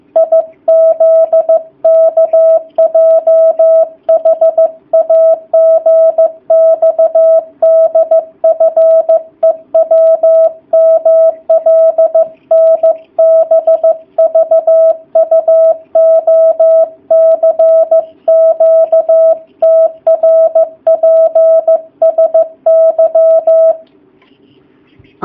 Morse code, heard as a keyed tone, about 15 wpm, 660 Hz. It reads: IZKJHAGXDFEWMLNBVUOCQTRPSY
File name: jh8cbh / 2019cw / sp15- w